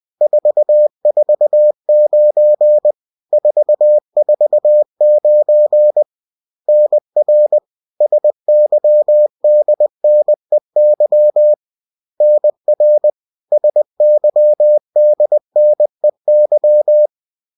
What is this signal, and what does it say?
Morse code, heard as a keyed tone, about 20 wpm, 600 Hz. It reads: 449 449 NR SYDNEY NR SYDNEY